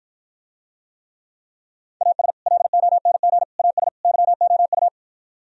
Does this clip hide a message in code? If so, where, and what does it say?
Yes: Morse code from 2.0 to 4.9 s, heard as a keyed tone, about 52 wpm, 700 Hz. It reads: AS LONG AS YOU